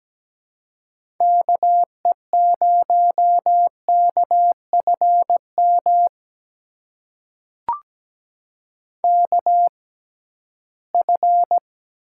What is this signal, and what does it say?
Morse code, heard as a keyed tone, about 17 wpm, spaced as for 13 wpm, 700 Hz. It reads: KE0KFM KF
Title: USA Call Signs - 6 Characters - 6x Character Spacing - 17wpm